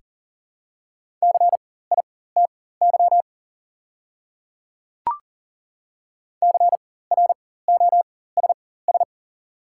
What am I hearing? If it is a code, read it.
Morse code, heard as a keyed tone, about 40 wpm, spaced as for 19 wpm, 690 Hz. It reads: CITY CROSS
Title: 4x Character Spacing - Top 300 Words - 40wpm